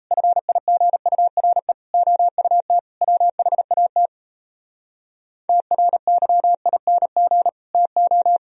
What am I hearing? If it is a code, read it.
Morse code, heard as a keyed tone, about 37 wpm, 700 Hz. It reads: FIGURE OUT WHAT TRYING TO